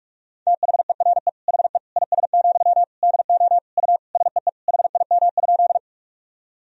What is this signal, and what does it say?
Morse code, heard as a keyed tone, about 45 wpm, 700 Hz. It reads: THERE HE IS, DO U SEE HIM?
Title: Sentences from Top 100 Words and Abbreviations 45wpm